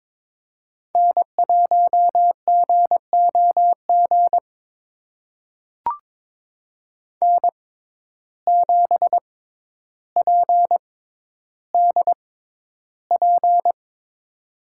Morse code, heard as a keyed tone, about 21 wpm, spaced as for 11 wpm, 700 Hz. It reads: N1GOG N7PDP
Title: USA Call Signs - 5 Characters - 6x Character Spacing - 22wpm